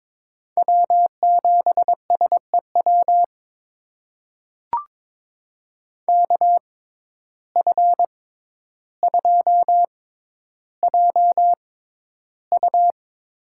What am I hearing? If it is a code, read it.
Morse code, heard as a keyed tone, about 22 wpm, spaced as for 11 wpm, 700 Hz. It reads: W7SEW KF2JU
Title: USA Call Signs - 5 Characters - 6x Character Spacing - 22wpm